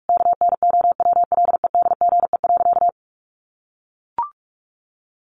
Morse code, heard as a keyed tone, about 44 wpm, 700 Hz. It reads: KNOWLEDGE.
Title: Sentences from Top 1100 Words 45wpm